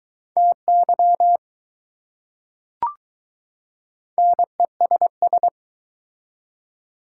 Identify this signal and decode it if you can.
Morse code, heard as a keyed tone, about 21 wpm, 700 Hz. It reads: TY NESS